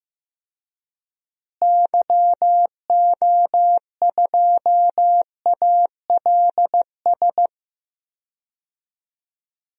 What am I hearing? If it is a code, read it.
Morse code, heard as a keyed tone, about 16 wpm, 700 Hz. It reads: YO2ALS
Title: DXCC-100 Call Signs - 6 Characters - Repeated 3x - 15wpm